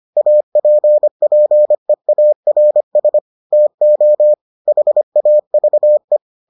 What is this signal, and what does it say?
Morse code, heard as a keyed tone, about 25 wpm, 600 Hz. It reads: APPEARS TO HAVE